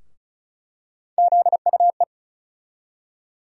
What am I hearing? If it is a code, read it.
Morse code, heard as a keyed tone, about 36 wpm, 700 Hz. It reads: ZUE
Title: USA Call Sign Postfixes - 3 Letters 35wpm